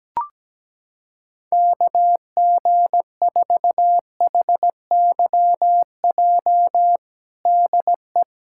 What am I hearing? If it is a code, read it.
Morse code, heard as a keyed tone, about 17 wpm, 700 Hz. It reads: KG4HYJ DE
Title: USA Callsign DE Callsign - 6 Characters - 17wpm